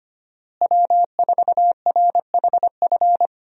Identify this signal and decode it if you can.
Morse code, heard as a keyed tone, about 25 wpm, 700 Hz. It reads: W4RHF
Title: USA Call Signs - 5 Characters - Repeated 3x - 25wpm